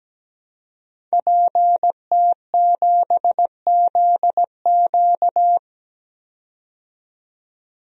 Morse code, heard as a keyed tone, about 17 wpm, 700 Hz. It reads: PT7ZQ
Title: DXCC-100 Call Signs - 5 Characters - 17wpm